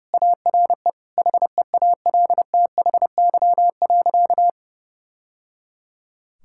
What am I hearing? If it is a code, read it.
Morse code, heard as a keyed tone, about 30 wpm, 700 Hz. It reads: ARE HEALTHY.
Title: Sentences from Top 1000 Words - Review Part 5 - 30wpm